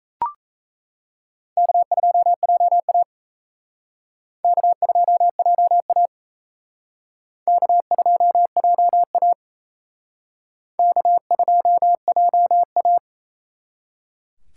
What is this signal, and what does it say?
Morse code, heard as a keyed tone, about 35 wpm, 700 Hz. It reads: K2JA K2JA K2JA K2JA
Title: Speed Racing - USA 1x2 Call Signs 28wpm